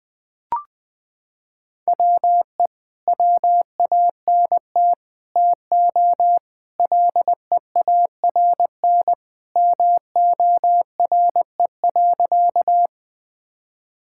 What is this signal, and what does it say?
Morse code, heard as a keyed tone, about 21 wpm, 700 Hz. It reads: WE WANT TO LEARN MORE.
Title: Sentences with 3-Word Phrases - 20wpm